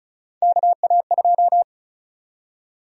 Morse code, heard as a keyed tone, about 36 wpm, 700 Hz. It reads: KA2
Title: USA Call Sign Prefixes - 35wpm